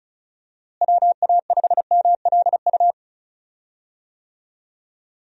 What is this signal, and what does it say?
Morse code, heard as a keyed tone, about 36 wpm, 700 Hz. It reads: WA5MLU
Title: USA Call Signs - 6 Characters - Repeated 3x - 35wpm